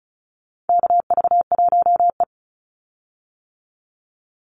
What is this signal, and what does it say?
Morse code, heard as a keyed tone, about 34 wpm, 700 Hz. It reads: KV1E
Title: Call Signs - 35wpm